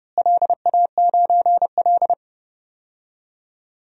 Morse code, heard as a keyed tone, about 30 wpm, 700 Hz. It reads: LA9L
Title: DXCC-100 Call Signs - 4 Characters - Repeated 3x - 30wpm